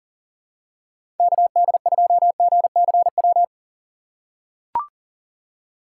Morse code, heard as a keyed tone, about 40 wpm, 700 Hz. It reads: KD2GCW
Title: USA Call Signs - 6 Characters - 3x Character Spacing - 40wpm